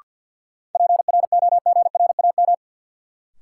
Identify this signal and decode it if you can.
Morse code, heard as a keyed tone, about 52 wpm, 700 Hz. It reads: PROGRAM
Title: Sentences from Top 1000 Words - Review Part 4 - New Word Drill - 50wpm